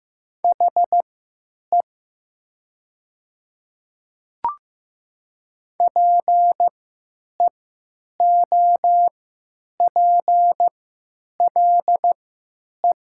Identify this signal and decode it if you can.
Morse code, heard as a keyed tone, about 15 wpm, spaced as for 9 wpm, 700 Hz. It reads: HE PEOPLE
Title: 3x Character Spacing - Top 100 Words - 15wpm